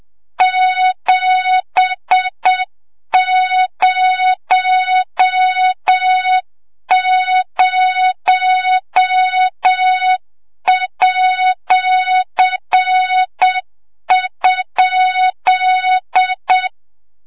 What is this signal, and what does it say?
Morse code, heard as a keyed tone, about 7 wpm, 3700 Hz, with noise. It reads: 700@?
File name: Morze-HA2MN / hang / 25 GYAKORLAT